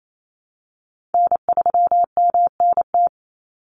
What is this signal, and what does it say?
Morse code, heard as a keyed tone, about 28 wpm, 700 Hz. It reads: N3MNT
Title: Call Signs Repeated 2x - 28wpm